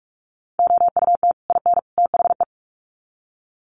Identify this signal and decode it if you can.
Morse code, heard as a keyed tone, about 44 wpm, 700 Hz. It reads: OUT IN THE
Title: Sets of 3 Words 45wpm